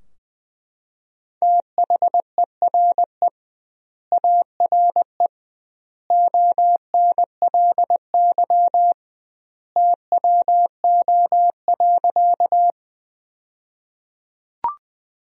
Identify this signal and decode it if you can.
Morse code, heard as a keyed tone, about 20 wpm, 700 Hz. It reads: THERE ARE ONLY TWO.
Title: 2x Word-Spacing - Sentences from Top 200 Words - 20wpm